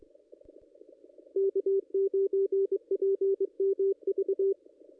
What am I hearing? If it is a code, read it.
Morse code, heard as a keyed tone, about 22 wpm, 390 Hz. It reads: K9PMV
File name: contests / ss-cw-k5tr-op-n5rz -1800